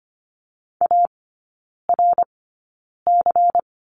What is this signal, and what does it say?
Morse code, heard as a keyed tone, about 25 wpm, spaced as for 9 wpm, 700 Hz. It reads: ARC
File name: course / 25:10wpm / 100 Lesson 17e - Letter u - Three Characters 25wpm